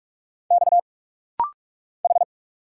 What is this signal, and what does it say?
Morse code, heard as a keyed tone, about 46 wpm, 700 Hz. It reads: X H
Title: Single Letters - Rapid-Fire 45wpm